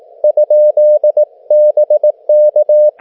Morse code, heard as a keyed tone, about 18 wpm, 600 Hz, with radio noise. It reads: ?BK